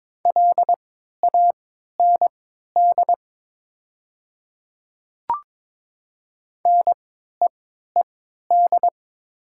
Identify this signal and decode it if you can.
Morse code, heard as a keyed tone, about 22 wpm, spaced as for 13 wpm, 700 Hz. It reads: LAND NEED
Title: 3x Character Spacing - Top 200 Words - 22wpm